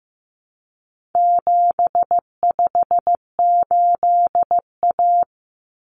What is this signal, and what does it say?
Morse code, heard as a keyed tone, about 15 wpm, 700 Hz. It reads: 758A
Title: Common Cut Numbers - 4-Digit 15wpm